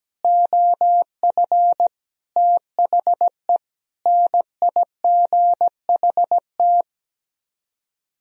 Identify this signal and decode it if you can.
Morse code, heard as a keyed tone, about 17 wpm, 700 Hz. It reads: OF THE NIGHT
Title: Speed-Racing - Sets of 4 Words 17wpm